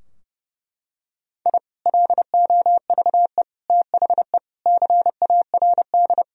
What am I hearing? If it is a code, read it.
Morse code, heard as a keyed tone, about 32 wpm, 700 Hz. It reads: I LOVE THE CARD
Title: Sentences from Top 1000 Words - Review Part 4 - 30wpm